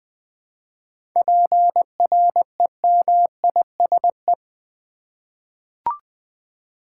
Morse code, heard as a keyed tone, about 20 wpm, 700 Hz. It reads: PREMISE
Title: 7 Letter Words - First 3 Letters Given - 20wpm